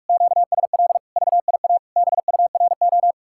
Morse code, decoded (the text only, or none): QSL VIA BURO